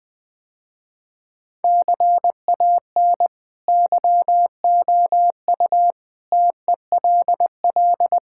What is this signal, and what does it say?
Morse code, heard as a keyed tone, about 20 wpm, 700 Hz. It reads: CAN YOU TELL
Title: Sentences with 8-Word Phrases - Spoken Beginning - 20wpm